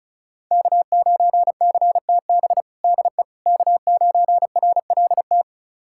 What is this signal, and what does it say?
Morse code, heard as a keyed tone, about 36 wpm, 700 Hz. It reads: K9CTB DE K9RLT
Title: USA Callsign DE Callsign - 5 Characters - 35wpm